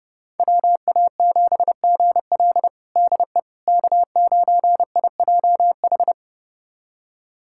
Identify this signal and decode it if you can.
Morse code, heard as a keyed tone, about 29 wpm, 700 Hz. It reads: WA7GL DE K9IJH
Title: USA Callsign DE Callsign - 5 Characters - 30wpm